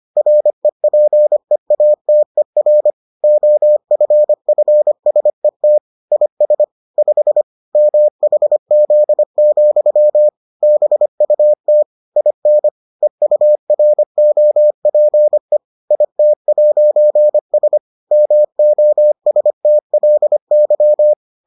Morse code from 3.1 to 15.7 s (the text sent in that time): OFFSET IS 5 MHZ, BUT IN EUROPE